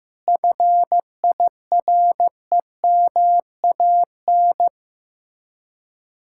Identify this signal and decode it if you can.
Morse code, heard as a keyed tone, about 15 wpm, 700 Hz. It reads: FIREMAN
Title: Speed-Racing Common QSO Phrases Encore - 15wpm